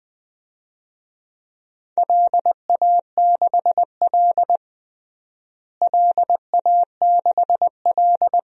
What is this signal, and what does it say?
Morse code, heard as a keyed tone, about 20 wpm, 700 Hz. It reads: LA6L LA6L
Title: DXCC-100 Call Signs - 4 Characters - Repeated 3x - 20wpm